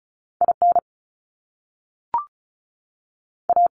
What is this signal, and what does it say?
Morse code, heard as a keyed tone, about 36 wpm, 700 Hz. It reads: IN A
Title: Top Two Letter Combinations in Words 35wpm